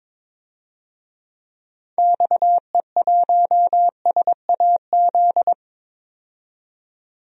Morse code, heard as a keyed tone, about 22 wpm, 700 Hz. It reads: XE1SAZ
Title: DXCC-100 Call Signs - 6 Characters - 22wpm